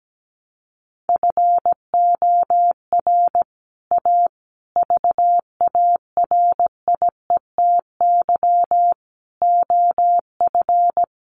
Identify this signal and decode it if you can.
Morse code, heard as a keyed tone, about 17 wpm, 700 Hz. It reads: FOR A VARIETY OF
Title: Sets of 5 Words Encore 17wpm